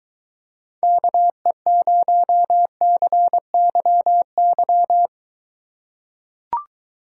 Morse code, heard as a keyed tone, about 23 wpm, 700 Hz. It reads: KE0CYY